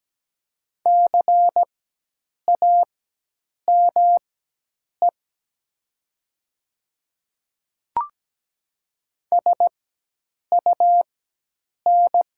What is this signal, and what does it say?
Morse code, heard as a keyed tone, about 17 wpm, spaced as for 8 wpm, 700 Hz. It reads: CAME SUN
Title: 4x Character Spacing - Top 300 Words - 17wpm